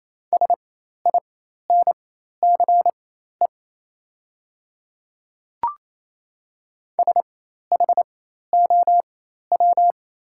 Morse code, heard as a keyed tone, about 28 wpm, spaced as for 13 wpm, 700 Hz. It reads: SINCE SHOW
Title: 4x Character Spacing - Top 300 Words - 28wpm